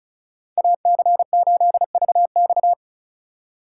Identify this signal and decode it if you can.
Morse code, heard as a keyed tone, about 36 wpm, 700 Hz. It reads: AC8VX
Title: USA Call Signs - 3x Character Spacing - 35wpm